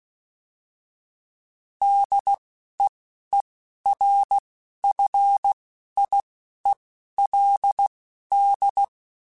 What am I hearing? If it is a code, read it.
Morse code, heard as a keyed tone, about 16 wpm, spaced as for 11 wpm, 800 Hz. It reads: DEERFIELD
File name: fromSMs / FromSMs mp3s / Cities@10